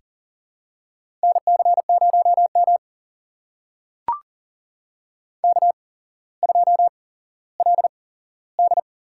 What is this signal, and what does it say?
Morse code, heard as a keyed tone, about 39 wpm, spaced as for 17 wpm, 700 Hz. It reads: NC0M K2LD
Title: USA Call Signs - 4 Characters - 8x Character Spacing - 40wpm